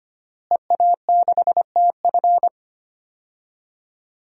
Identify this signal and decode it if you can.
Morse code, heard as a keyed tone, about 25 wpm, 700 Hz. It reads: EA6TF
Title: DXCC-100 Call Signs - 5 Characters - Repeated 3x - 25wpm